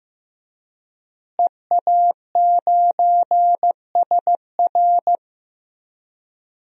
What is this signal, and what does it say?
Morse code, heard as a keyed tone, about 15 wpm, 700 Hz. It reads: EA9SR
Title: DXCC-100 Call Signs - 5 Characters - Repeated 2x - 15wpm